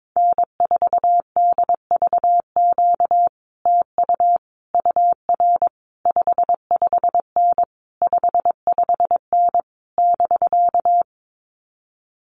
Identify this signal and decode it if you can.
Morse code, heard as a keyed tone, about 22 wpm, 700 Hz. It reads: N4DVQ TU UR 55N 55N <BK>
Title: Common QSO Phrases Encore 22wpm